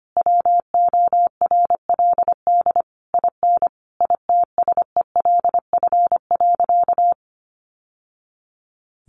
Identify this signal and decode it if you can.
Morse code, heard as a keyed tone, about 25 wpm, 700 Hz. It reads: WORLD IN ITSELF.
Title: Sentences from Top 900 Words 25wpm